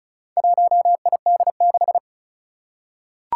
Morse code, heard as a keyed tone, about 36 wpm, 700 Hz. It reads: 1ID6
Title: Four Challenging Letter-Number Combinations 35wpm